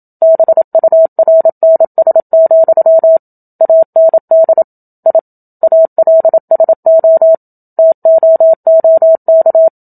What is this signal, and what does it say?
Morse code, heard as a keyed tone, about 27 wpm, 650 Hz. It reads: BURNS, AND I ALSO TOOK
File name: Book Two / 01-Chapter 1 - 27 WPM